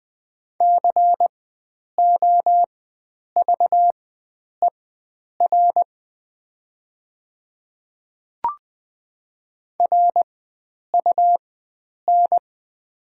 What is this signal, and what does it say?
Morse code, heard as a keyed tone, about 20 wpm, spaced as for 9 wpm, 700 Hz. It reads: COVER RUN
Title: 4x Character Spacing - Top 300 Words - 20wpm